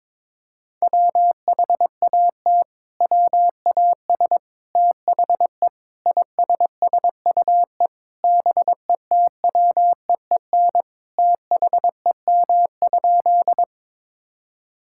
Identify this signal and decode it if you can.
Morse code, heard as a keyed tone, about 22 wpm, 700 Hz. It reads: WHAT WAS THE ISSUE BETWEEN THEM?
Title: Sentences from Top 1000 Words - Review Part 6 - 22wpm